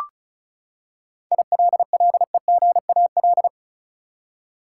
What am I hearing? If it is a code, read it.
Morse code, heard as a keyed tone, about 34 wpm, 700 Hz. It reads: ILLEGAL
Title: Seven Letter Words - 35wpm